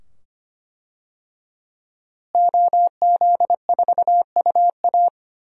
Morse code, heard as a keyed tone, about 25 wpm, 700 Hz. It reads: OZ4UA
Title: DXCC-100 Call Signs - 5 Characters - Repeated 2x - 25wpm